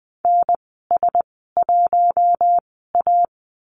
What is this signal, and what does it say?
Morse code, heard as a keyed tone, about 20 wpm, spaced as for 14 wpm, 700 Hz. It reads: NS1A
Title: 042 Lesson 8g - Number 1 - Call Signs 20wpm